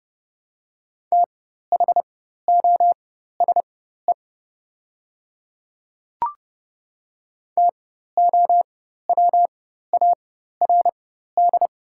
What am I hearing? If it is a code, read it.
Morse code, heard as a keyed tone, about 30 wpm, spaced as for 14 wpm, 700 Hz. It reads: THOSE TOWARD
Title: 4x Character Spacing - Top 500 Words - 30wpm